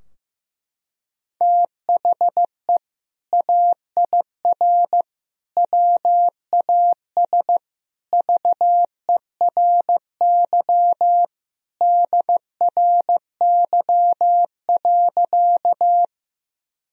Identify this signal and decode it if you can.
Morse code, heard as a keyed tone, about 15 wpm, 700 Hz. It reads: THE AIR WAS VERY DRY.
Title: Sentences from Top 1000 Words - Review Part 3 - 15wpm